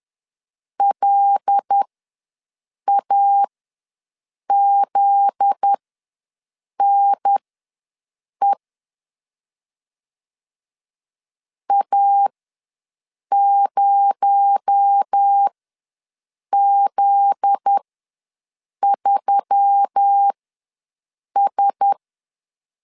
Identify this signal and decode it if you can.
Morse code, heard as a keyed tone, about 10 wpm, spaced as for 6 wpm, 800 Hz. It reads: LAZNE A0Z3S
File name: LEZIONI / 7 GF9Z / 7 GF9Z ws 10wpm 3919